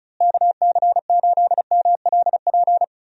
Morse code, heard as a keyed tone, about 36 wpm, 700 Hz. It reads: KC8MLP
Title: USA Call Signs - 6 Characters - Repeated 3x - 35wpm